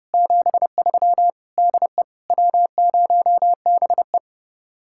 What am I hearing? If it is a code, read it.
Morse code, heard as a keyed tone, about 37 wpm, 700 Hz. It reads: 73 DE W0BE
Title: Speed-Racing Common QSO Phrases - 30wpm